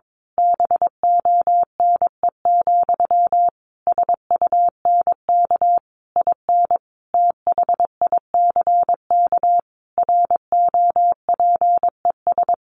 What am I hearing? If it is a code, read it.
Morse code, heard as a keyed tone, about 22 wpm, 700 Hz. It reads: BONE, SUNK IN THICK ROPES
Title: A Sound Of Thunder 22wpm